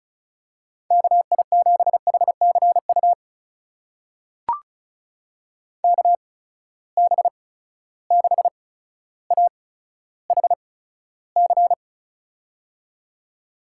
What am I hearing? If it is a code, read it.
Morse code, heard as a keyed tone, about 36 wpm, spaced as for 15 wpm, 700 Hz. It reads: KI7HCU KB6AHC
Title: USA Call Signs - 6 Characters - 8x Character Spacing - 35wpm